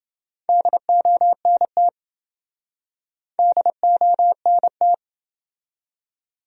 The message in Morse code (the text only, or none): DONT DONT